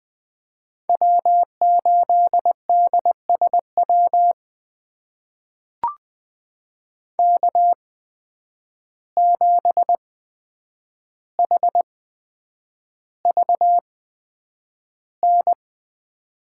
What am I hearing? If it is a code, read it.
Morse code, heard as a keyed tone, about 20 wpm, spaced as for 9 wpm, 700 Hz. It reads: W8DSW K7HVN
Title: USA Call Signs - 5 Characters - 8x Character Spacing - 20wpm